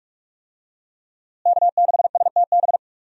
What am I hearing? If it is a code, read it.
Morse code, heard as a keyed tone, about 46 wpm, 700 Hz. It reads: K6STB
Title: USA Call Signs - 5 Characters - Repeated 3x - 45wpm